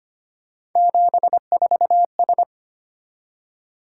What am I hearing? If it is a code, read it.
Morse code, heard as a keyed tone, about 25 wpm, 700 Hz. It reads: 74S
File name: Three Letter-Number Combinations - 25wpm